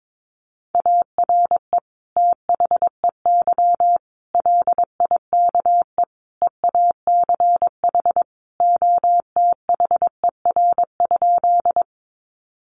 Morse code, heard as a keyed tone, about 22 wpm, 700 Hz. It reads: ARE THEY LIKE EACH OTHER?
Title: Sentences from Top 200 Words 22wpm